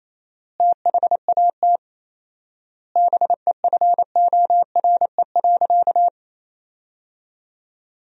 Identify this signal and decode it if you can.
Morse code, heard as a keyed tone, about 28 wpm, 700 Hz. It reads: THAT BEFORE.
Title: 4x Word-Spacing - Sentences from Top 400 Words - 28wpm